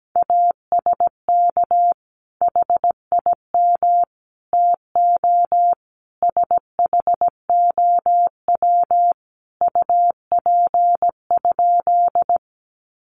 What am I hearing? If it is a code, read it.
Morse code, heard as a keyed tone, about 17 wpm, 700 Hz. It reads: ASK HIM TO SHOW UP?
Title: Sentences from Top 200 Words 17wpm